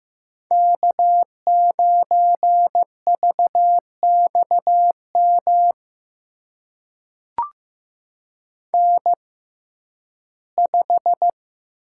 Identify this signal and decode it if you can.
Morse code, heard as a keyed tone, about 15 wpm, spaced as for 11 wpm, 700 Hz. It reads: K9VXM N5
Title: USA Call Signs - 5 Characters - 6x Character Spacing - 15wpm